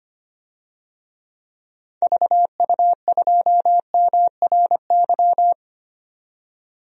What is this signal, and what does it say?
Morse code, heard as a keyed tone, about 25 wpm, 700 Hz. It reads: VU2MRY
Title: DXCC-100 Call Signs - 6 Characters - 25wpm